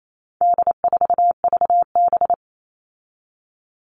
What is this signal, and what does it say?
Morse code, heard as a keyed tone, about 28 wpm, 700 Hz. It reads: D4VB
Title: All Cut Numbers - 4-Digit 28wpm